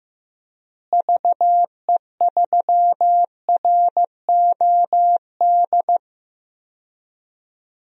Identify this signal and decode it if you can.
Morse code, heard as a keyed tone, about 15 wpm, 700 Hz. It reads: VE3ROD
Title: DXCC-100 Call Signs - 6 Characters - Repeated 3x - 15wpm